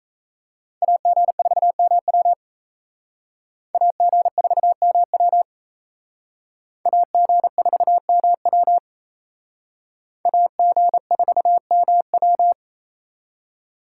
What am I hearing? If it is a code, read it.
Morse code, heard as a keyed tone, about 36 wpm, 700 Hz. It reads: AG4MW AG4MW AG4MW AG4MW